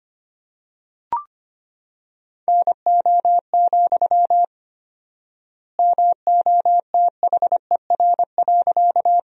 Morse code, heard as a keyed tone, about 25 wpm, 700 Hz. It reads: NO, MOTHER.
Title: 4x Word-Spacing - Sentences from Top 200 Words - 25wpm